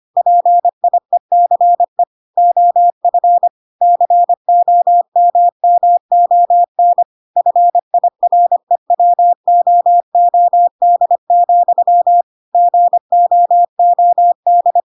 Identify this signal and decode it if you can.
Morse code, heard as a keyed tone, about 25 wpm, 700 Hz. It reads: PIECE OF COMMON FIREWOOD, GOOD
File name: Pinocchio / Pinocchio ch1 25 25 ik1hge